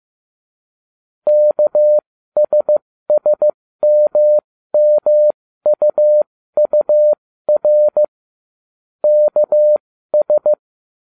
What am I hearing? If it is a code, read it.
Morse code, heard as a keyed tone, about 15 wpm, spaced as for 13 wpm, 600 Hz. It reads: KSSMMUUR KS